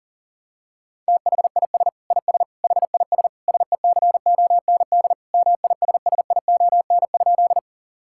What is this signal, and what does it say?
Morse code, heard as a keyed tone, about 40 wpm, 700 Hz. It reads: THIS IS HIS SECOND MISSION?
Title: Sentences from Top 1600 Words 40wpm